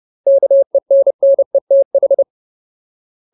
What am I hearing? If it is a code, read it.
Morse code, heard as a keyed tone, about 30 wpm, 550 Hz. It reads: KENNETH
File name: Ultimate Word List / 30WPM Ultimate